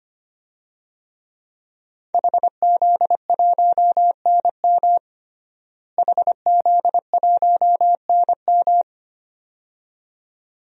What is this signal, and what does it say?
Morse code, heard as a keyed tone, about 25 wpm, 700 Hz. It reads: HZ1NM HZ1NM
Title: DXCC-100 Call Signs - 5 Characters - Repeated 2x - 25wpm